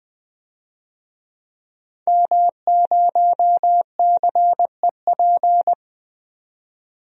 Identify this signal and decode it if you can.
Morse code, heard as a keyed tone, about 21 wpm, 700 Hz. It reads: M0CEP
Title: DXCC-100 Call Signs - 5 Characters - 20wpm